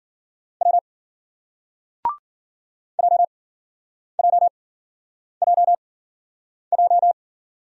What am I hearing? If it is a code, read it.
Morse code, heard as a keyed tone, about 56 wpm, spaced as for 9 wpm, 700 Hz. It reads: W JJJJ